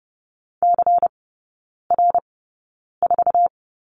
Morse code, heard as a keyed tone, about 30 wpm, spaced as for 9 wpm, 690 Hz. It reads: CR4
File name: Lesson 16e - Letter c - Three Characters 30wpm